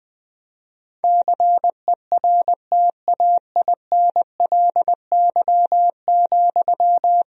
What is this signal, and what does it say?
Morse code, heard as a keyed tone, about 20 wpm, 700 Hz. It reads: CERTAINLY,
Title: Sentences from Top 1000 Words - Review Part 1 - 20wpm